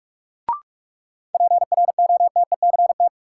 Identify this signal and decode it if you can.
Morse code, heard as a keyed tone, about 46 wpm, 700 Hz. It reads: PROTECT
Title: Sentences from Top 900 Words - New Word Drill - 45wpm